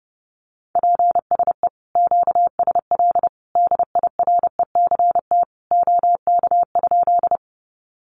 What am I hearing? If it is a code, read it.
Morse code, heard as a keyed tone, about 28 wpm, 700 Hz. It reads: PSE QSL DIRECT OK?
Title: Common QSO Phrases 30wpm